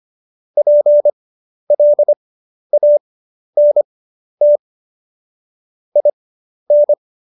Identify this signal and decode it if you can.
Morse code, heard as a keyed tone, about 25 wpm, spaced as for 11 wpm, 590 Hz. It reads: PLANT IN
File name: Jokes-25-char-10-eff